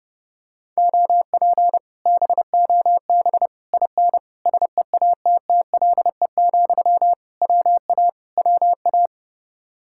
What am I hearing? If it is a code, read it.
Morse code, heard as a keyed tone, about 30 wpm, 700 Hz. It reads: OP BOB IN SEATTLE, WA WA